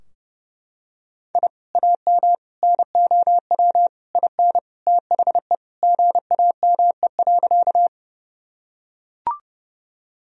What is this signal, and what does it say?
Morse code, heard as a keyed tone, about 29 wpm, 700 Hz. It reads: I AM NOW IN THE GAME.
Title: Sentences from Top 1000 Words - Review Part 3 - 30wpm